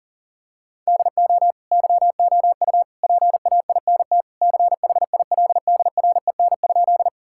Morse code, heard as a keyed tone, about 40 wpm, 700 Hz. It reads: DO YOU PAINT CHILDREN?
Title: Sentences from Top 1000 Words - Review Part 6 - 40wpm